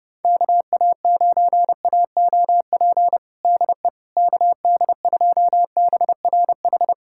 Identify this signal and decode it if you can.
Morse code, heard as a keyed tone, about 32 wpm, 700 Hz. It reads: KA9AOP DE KD2BRH
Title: USA Callsign DE Callsign - 6 Characters - 30wpm